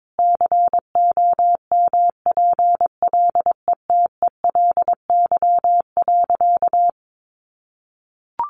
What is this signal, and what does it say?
Morse code, heard as a keyed tone, about 22 wpm, 700 Hz. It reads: COMPLETELY.
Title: Sentences from Top 1400 Words 22wpm